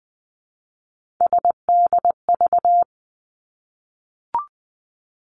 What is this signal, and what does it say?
Morse code, heard as a keyed tone, about 20 wpm, 700 Hz. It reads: SDV
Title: Three Challenging Letter-Number Combinations 20wpm